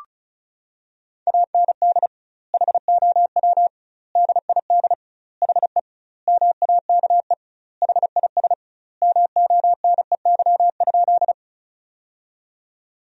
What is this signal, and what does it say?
Morse code, heard as a keyed tone, about 36 wpm, 700 Hz. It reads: AND HOW DID HE MAKE HIS MONEY?
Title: 2x Word-Spacing - Sentences from Top 400 Words - 35wpm